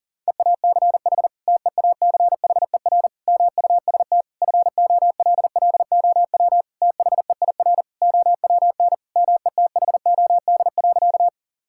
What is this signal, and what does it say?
Morse code, heard as a keyed tone, about 37 wpm, 700 Hz. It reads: EACH TEACHER MUST FOLLOW THEIR OWN METHOD.
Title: Sentences from Top 1000 Words - Review Part 6 - 40wpm